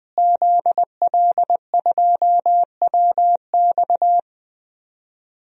Morse code, heard as a keyed tone, about 20 wpm, 700 Hz. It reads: ZL2WX
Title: DXCC-100 Call Signs - 5 Characters - Repeated 2x - 20wpm